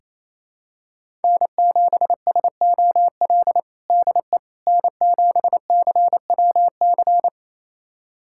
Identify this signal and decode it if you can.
Morse code, heard as a keyed tone, about 27 wpm, 700 Hz. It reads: N7SOL DE N7CWC
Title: USA Callsign DE Callsign - 5 Characters - 28wpm